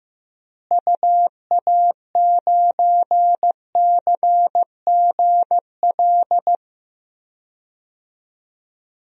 Morse code, heard as a keyed tone, about 15 wpm, 700 Hz. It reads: UA9CGL
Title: DXCC-100 Call Signs - 6 Characters - Repeated 3x - 15wpm